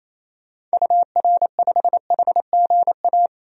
Morse code, heard as a keyed tone, about 28 wpm, 700 Hz. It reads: UR5HGA
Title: DXCC-100 Call Signs - 6 Characters - 28wpm